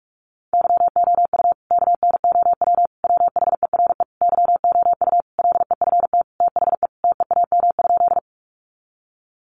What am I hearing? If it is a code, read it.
Morse code, heard as a keyed tone, about 46 wpm, 700 Hz. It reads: YOU KNOW WHERE YOU LEFT THE TEAM?
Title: Sentences from Top 1500 Words 45wpm